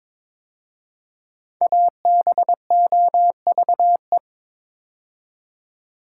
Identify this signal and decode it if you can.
Morse code, heard as a keyed tone, about 22 wpm, 700 Hz. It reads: ABOVE
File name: 5 Letter Words - First 3 Letters Given - 22wpm